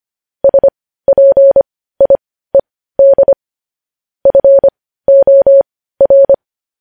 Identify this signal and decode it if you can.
Morse code, heard as a keyed tone, about 25 wpm, spaced as for 15 wpm, 550 Hz. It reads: SPIED FOR
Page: News Headlines in Morse Code at 15 WPM